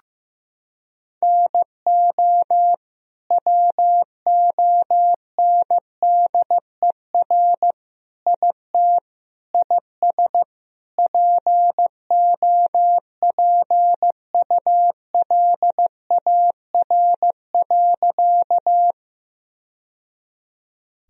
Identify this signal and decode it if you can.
Morse code, heard as a keyed tone, about 15 wpm, 700 Hz. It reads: NO WONDER IT IS POPULAR.
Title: Sentences from Top 1000 Words - Review Part 3 - 15wpm